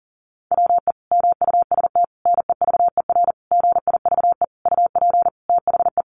Morse code, heard as a keyed tone, about 40 wpm, 700 Hz. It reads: WE MUST NEVER GIVE UP THE
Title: Sentences from Top 1500 Words 40wpm